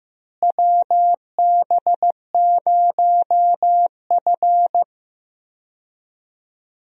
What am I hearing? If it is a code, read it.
Morse code, heard as a keyed tone, about 16 wpm, 700 Hz. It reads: WB0F
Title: USA Call Signs - 4 Characters - Repeated 2x - 15wpm